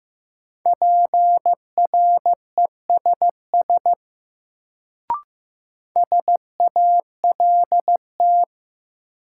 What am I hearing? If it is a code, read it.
Morse code, heard as a keyed tone, about 15 wpm, 700 Hz. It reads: PRESS SALT
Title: Sentences from Top 800 Words - New Word Drill - 15wpm